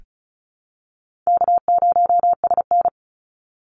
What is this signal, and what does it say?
Morse code, heard as a keyed tone, about 34 wpm, 700 Hz. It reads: K0SN
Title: Easy Call Signs Repeated 2x - 35wpm